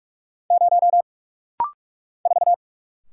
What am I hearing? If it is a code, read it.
Morse code, heard as a keyed tone, about 44 wpm, 700 Hz. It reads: 0 4
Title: Single Letter-Number Alt - Rapid-Fire 45wpm